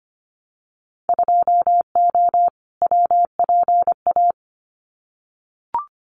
Morse code, heard as a keyed tone, about 25 wpm, 700 Hz. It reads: 2O WPA
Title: Winter Field Day Exchanges 25wpm